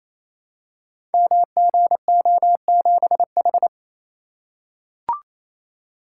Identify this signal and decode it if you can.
Morse code, heard as a keyed tone, about 28 wpm, 700 Hz. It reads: MGO7H
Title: Code Groups - 5 Letter:Numbers - 28wpm